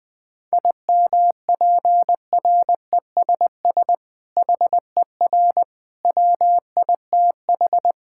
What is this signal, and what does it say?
Morse code, heard as a keyed tone, about 20 wpm, 700 Hz. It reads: IMPRESS HER WITH
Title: Sentences from Top 1600 Words 20wpm